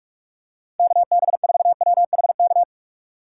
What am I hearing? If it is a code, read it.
Morse code, heard as a keyed tone, about 43 wpm, 700 Hz. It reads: KB4WHK
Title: USA Call Signs - 6 Characters - 2x Character Spacing - 45wpm